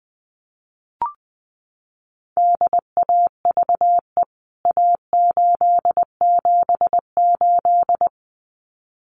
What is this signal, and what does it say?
Morse code, heard as a keyed tone, about 20 wpm, 700 Hz. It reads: DAVE A878
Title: CWT Exchange 20wpm